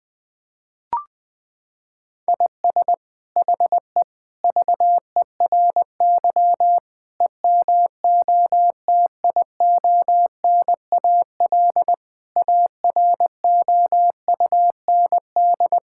Morse code, heard as a keyed tone, about 20 wpm, 700 Hz. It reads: IS HE VERY EMOTIONAL AROUND
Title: Sentences from Top 1000 Words - Review Part 5 - 20wpm